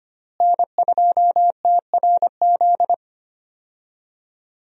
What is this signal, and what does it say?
Morse code, heard as a keyed tone, about 25 wpm, 700 Hz. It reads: N2TRZ
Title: USA Call Signs - 5 Characters - Repeated 2x - 25wpm